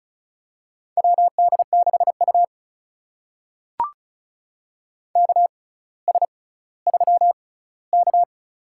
Morse code, heard as a keyed tone, about 36 wpm, spaced as for 18 wpm, 700 Hz. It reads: WD6U KS3K
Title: USA Call Signs - 4 Characters - 6x Character Spacing - 35wpm